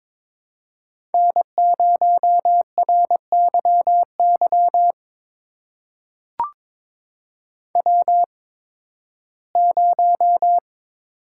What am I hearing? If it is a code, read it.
Morse code, heard as a keyed tone, about 23 wpm, spaced as for 15 wpm, 700 Hz. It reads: N0RYY W0
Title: USA Call Signs - 5 Characters - 8x Character Spacing - 22wpm